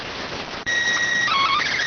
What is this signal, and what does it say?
SSTV (slow-scan television), PD 290.